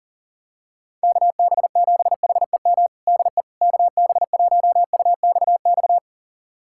Morse code, heard as a keyed tone, about 37 wpm, 700 Hz. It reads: KB7HEM DE KB1UXX
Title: USA Callsign DE Callsign - 6 Characters - 40wpm